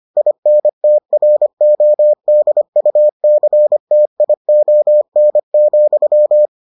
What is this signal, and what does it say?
Morse code, heard as a keyed tone, about 25 wpm, 600 Hz. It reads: INTRODUCTION,